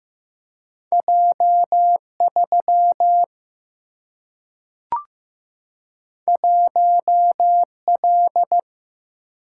Morse code, heard as a keyed tone, about 15 wpm, 700 Hz. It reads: J3 1L
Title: Single-Letter Single-Number 15wpm